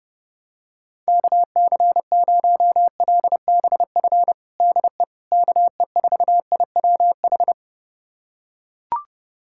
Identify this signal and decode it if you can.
Morse code, heard as a keyed tone, about 32 wpm, 700 Hz. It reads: KC0LBF DE KE4IWH
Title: USA Callsign DE Callsign - 6 Characters - 30wpm